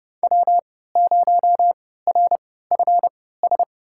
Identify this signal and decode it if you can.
Morse code, heard as a keyed tone, about 29 wpm, spaced as for 17 wpm, 700 Hz. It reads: W0RFS
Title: USA Call Signs - 3x Character Spacing - 30wpm